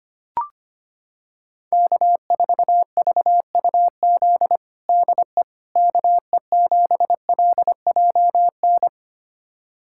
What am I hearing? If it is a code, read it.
Morse code, heard as a keyed tone, about 25 wpm, 700 Hz. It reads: K4VUZ DE KE7LJN
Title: USA Callsign DE Callsign 25wpm